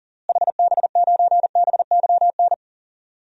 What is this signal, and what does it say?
Morse code, heard as a keyed tone, about 40 wpm, 700 Hz. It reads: HB9BYN
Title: DXCC-100 Call Signs - 6 Characters - Repeated 2x - 40wpm